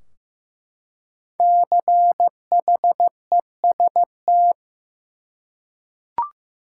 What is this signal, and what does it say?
Morse code, heard as a keyed tone, about 15 wpm, 700 Hz. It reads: CHEST
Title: Five Letter Words - 15wpm